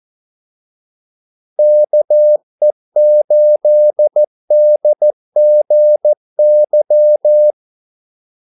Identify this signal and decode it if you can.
Morse code, heard as a keyed tone, about 14 wpm, 600 Hz. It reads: KE8DGY
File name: Callsigns-14wpm